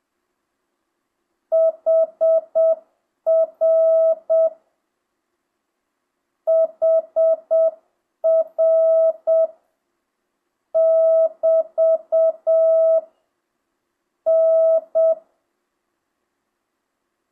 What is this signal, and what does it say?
Morse code, heard as a keyed tone, about 7 wpm, 1300 Hz. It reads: HR HR <BT> N
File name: jh8cbh / 2025cw